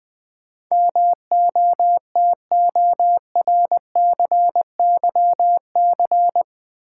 Morse code, heard as a keyed tone, about 20 wpm, 700 Hz. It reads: MOTORCYC
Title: Build A Word - 4 Letters At A Time - 10-Letter Words - 20wpm